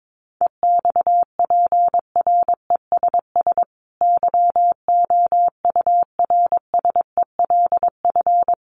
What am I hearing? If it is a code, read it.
Morse code, heard as a keyed tone, about 22 wpm, 700 Hz. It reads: EXPRESS YOURSELF
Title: Sentences from Top 1100 Words 22wpm